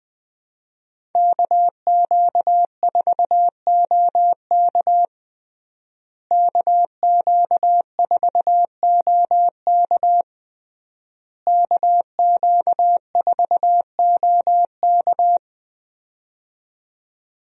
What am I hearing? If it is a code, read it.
Morse code, heard as a keyed tone, about 20 wpm, 700 Hz. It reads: KQ4OK KQ4OK KQ4OK
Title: USA Call Signs - 5 Characters - Repeated 3x - 20wpm